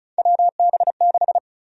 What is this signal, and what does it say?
Morse code, heard as a keyed tone, about 36 wpm, 700 Hz. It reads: WB6
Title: USA Call Sign Prefixes - Repeated 2x - 35wpm